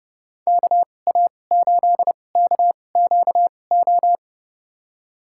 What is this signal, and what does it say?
Morse code, heard as a keyed tone, about 31 wpm, spaced as for 22 wpm, 700 Hz. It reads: KA8KQO
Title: USA Call Signs - 6 Characters - 2x Character Spacing - 30wpm